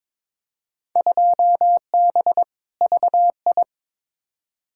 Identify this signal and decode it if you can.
Morse code, heard as a keyed tone, about 22 wpm, 700 Hz. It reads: 2B VI
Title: Field Day Exchanges Encore - 22wpm